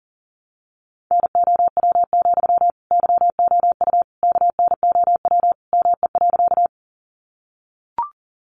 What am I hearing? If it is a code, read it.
Morse code, heard as a keyed tone, about 38 wpm, 700 Hz. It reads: NOW, YOU KNOW ME.
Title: Sentences from Top 200 Words 40wpm